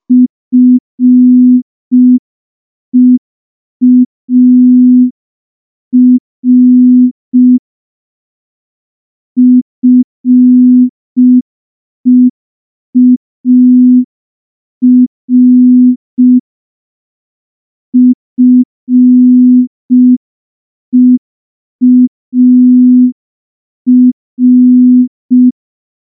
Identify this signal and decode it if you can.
Morse code, heard as a keyed tone, about 5 wpm, 250 Hz. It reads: FEAR FEAR FEAR
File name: morse / morse-training-05WPM